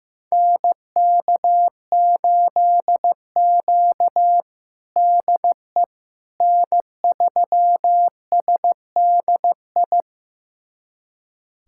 Morse code, heard as a keyed tone, about 15 wpm, 700 Hz. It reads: NK8Q DE N3SDI
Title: USA Callsign DE Callsign 15wpm